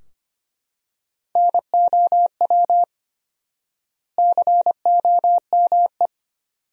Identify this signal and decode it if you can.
Morse code, heard as a keyed tone, about 25 wpm, 700 Hz. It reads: NOW COME
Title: 4x Word-Spacing - Sentences from Top 200 Words - 25wpm